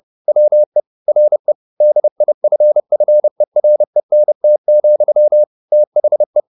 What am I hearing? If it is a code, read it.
Morse code, heard as a keyed tone, about 30 wpm, 600 Hz. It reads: WE RE DIFFERENT, THE
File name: the old man and the sea / the old man and the sea 1 30wpm